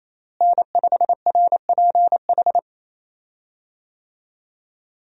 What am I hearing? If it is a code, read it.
Morse code, heard as a keyed tone, about 28 wpm, 700 Hz. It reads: N5RPH
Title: USA Call Signs - 5 Characters - Repeated 2x - 28wpm